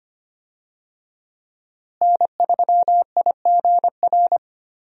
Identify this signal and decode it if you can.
Morse code, heard as a keyed tone, about 25 wpm, 700 Hz. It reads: N3IGR
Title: USA Call Signs - 5 Characters - Repeated 3x - 25wpm